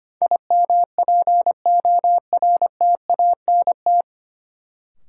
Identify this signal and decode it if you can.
Morse code, heard as a keyed tone, about 25 wpm, 700 Hz. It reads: IMPORTANT
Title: Sets of 3 Words - New Word Drill - 25wpm